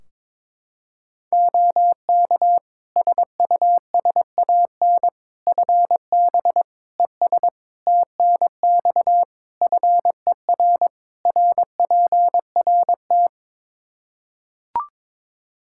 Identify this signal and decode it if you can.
Morse code, heard as a keyed tone, about 22 wpm, 700 Hz. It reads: OK SUSAN FB ES TNX FER RPRT
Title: Practice QSO - Exchange 2 - Info - 22wpm